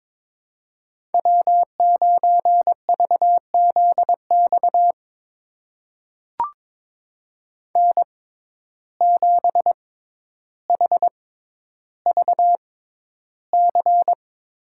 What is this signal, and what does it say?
Morse code, heard as a keyed tone, about 22 wpm, spaced as for 11 wpm, 700 Hz. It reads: W9VZX N7HVC
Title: USA Call Signs - 5 Characters - 6x Character Spacing - 22wpm